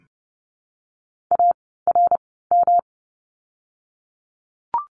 Morse code, heard as a keyed tone, about 30 wpm, spaced as for 17 wpm, 700 Hz. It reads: ARM
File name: course / 30:10wpm / 108 Lesson 18f - Letter m - Words 30wpm